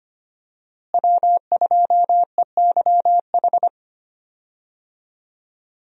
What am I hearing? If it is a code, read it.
Morse code, heard as a keyed tone, about 25 wpm, 700 Hz. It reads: W2EYH
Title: USA Call Signs - 5 Characters - Repeated 2x - 25wpm